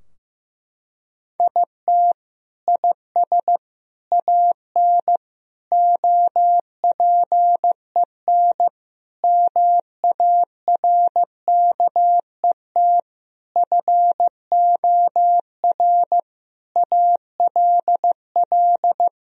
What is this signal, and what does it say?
Morse code, heard as a keyed tone, about 15 wpm, 700 Hz. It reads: IT IS AN OPEN MARKET FOR ALL
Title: Sentences from Top 1000 Words - Review Part 3 - 15wpm